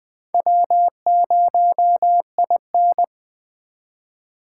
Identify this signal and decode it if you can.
Morse code, heard as a keyed tone, about 22 wpm, 700 Hz. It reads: W0IN